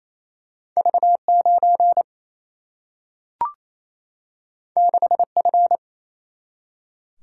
Morse code, heard as a keyed tone, about 28 wpm, 700 Hz. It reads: V9 6F
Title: Single-Letter Single-Number 28wpm